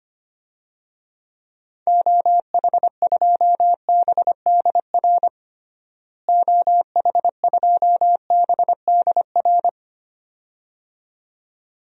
Morse code, heard as a keyed tone, about 25 wpm, 700 Hz. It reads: OH2BDR OH2BDR
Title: DXCC-100 Call Signs - 6 Characters - Repeated 2x - 25wpm